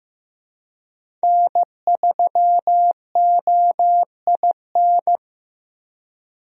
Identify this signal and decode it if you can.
Morse code, heard as a keyed tone, about 15 wpm, 700 Hz. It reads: N3OIN
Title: USA Call Signs - 5 Characters - Repeated 3x - 15wpm